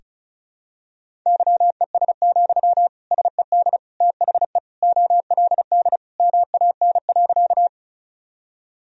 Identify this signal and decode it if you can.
Morse code, heard as a keyed tone, about 34 wpm, 700 Hz. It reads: YES, SED THE OLD MAN.
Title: Sentences from Top 500 Words and Abbreviations 35wpm